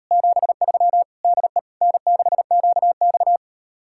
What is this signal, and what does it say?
Morse code, heard as a keyed tone, about 39 wpm, 700 Hz. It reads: 73 DE N6QX